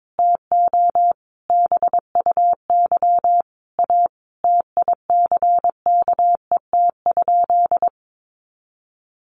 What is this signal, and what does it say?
Morse code, heard as a keyed tone, about 22 wpm, 700 Hz. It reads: TO BUY A TICKET?
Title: Sentences from Top 1400 Words 22wpm